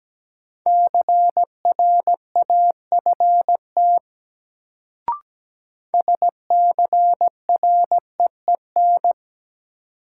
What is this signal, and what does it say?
Morse code, heard as a keyed tone, about 17 wpm, 700 Hz. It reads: CRAFT SCREEN
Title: Sentences from Top 1000 Words - Review Part 6 - New Word Drill - 17wpm